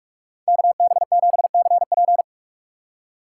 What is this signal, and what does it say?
Morse code, heard as a keyed tone, about 44 wpm, 700 Hz. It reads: KB7CP